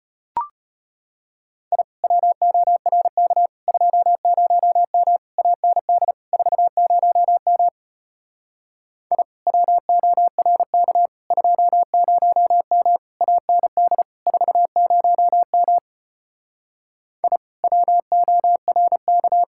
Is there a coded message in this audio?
Yes: Morse code from 1.7 to 19.4 s, heard as a keyed tone, about 33 wpm, 700 Hz. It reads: I WORK 20M AND 40M I WORK 20M AND 40M I WORK